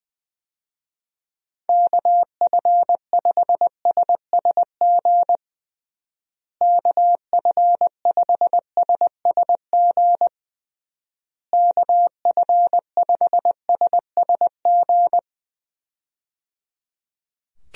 Morse code, heard as a keyed tone, about 20 wpm, 700 Hz. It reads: KF5SSG KF5SSG KF5SSG
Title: USA Call Signs - 6 Characters - Repeated 3x - 20wpm